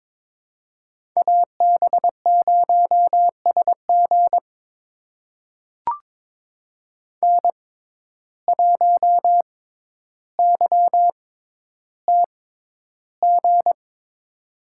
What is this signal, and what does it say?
Morse code, heard as a keyed tone, about 22 wpm, spaced as for 11 wpm, 700 Hz. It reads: AB0SG N1YTG
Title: USA Call Signs - 5 Characters - 6x Character Spacing - 22wpm